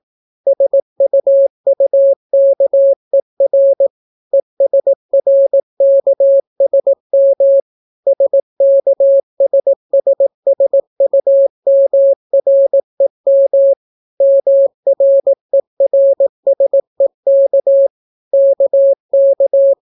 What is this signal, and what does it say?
Morse code, heard as a keyed tone, about 18 wpm, 550 Hz. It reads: SUUKER ESRKSM SKSSSUMREM MRERSEK KK